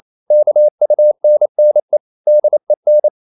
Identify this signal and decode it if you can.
Morse code, heard as a keyed tone, about 28 wpm, 600 Hz. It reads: KUNNE DEN
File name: Ferie QRQ 28wpm